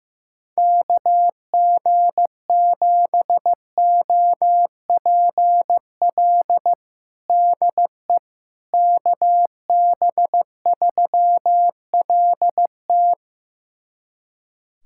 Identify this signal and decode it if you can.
Morse code, heard as a keyed tone, about 15 wpm, 700 Hz. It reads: KG7OPL DE KB3LT
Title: USA Callsign DE Callsign 15wpm